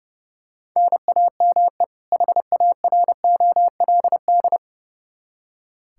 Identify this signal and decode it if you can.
Morse code, heard as a keyed tone, about 32 wpm, 700 Hz. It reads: NAME HAROLD
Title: Practice QSO - Exchange 1 - Beginning - 30wpm